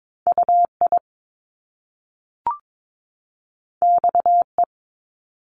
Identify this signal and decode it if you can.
Morse code, heard as a keyed tone, about 22 wpm, 700 Hz. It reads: UI XE
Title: Top Two Letter Combinations in Words 22wpm